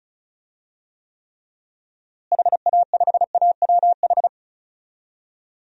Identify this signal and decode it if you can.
Morse code, heard as a keyed tone, about 36 wpm, 700 Hz. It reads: HA5AWH
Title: DXCC-100 Call Signs - 6 Characters - Repeated 3x - 35wpm